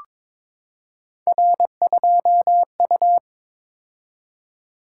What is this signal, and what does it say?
Morse code, heard as a keyed tone, about 22 wpm, 700 Hz. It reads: R2U
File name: practice / 22wpm / Three Letter-Number Combinations - 22wpm